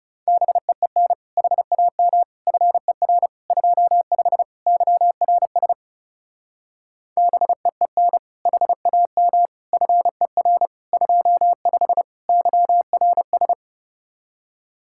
Morse code, heard as a keyed tone, about 39 wpm, 700 Hz. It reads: BEEN HAM FER 25 YRS BEEN HAM FER 25 YRS